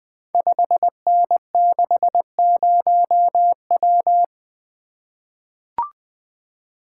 Morse code, heard as a keyed tone, about 20 wpm, 700 Hz. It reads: 5N60W
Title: Code Groups - 5 Letter:Numbers - 20wpm